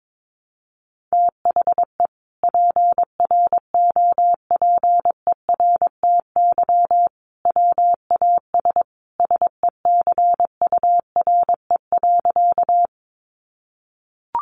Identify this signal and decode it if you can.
Morse code, heard as a keyed tone, about 22 wpm, 700 Hz. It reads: THE PROPERTY WAS SECURE.
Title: Sentences from Top 1200 Words 22wpm